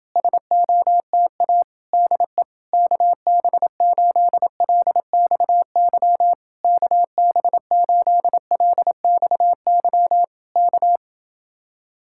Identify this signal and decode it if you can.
Morse code, heard as a keyed tone, about 27 wpm, 700 Hz. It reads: SOTA DE KB8LXY KB8LXY K